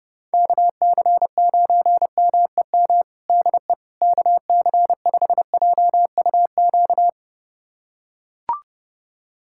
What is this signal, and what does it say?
Morse code, heard as a keyed tone, about 30 wpm, 700 Hz. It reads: KC9MEM DE KC5JUQ
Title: USA Callsign DE Callsign - 6 Characters - 30wpm